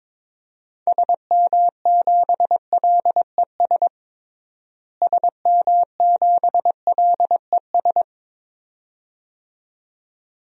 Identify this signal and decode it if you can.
Morse code, heard as a keyed tone, about 22 wpm, 700 Hz. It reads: SM7LES SM7LES
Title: DXCC-100 Call Signs - 6 Characters - Repeated 2x - 22wpm